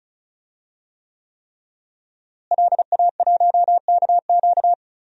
Morse code, heard as a keyed tone, about 36 wpm, 700 Hz. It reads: LA1KQ
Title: DXCC-100 Call Signs - 5 Characters - Repeated 3x - 35wpm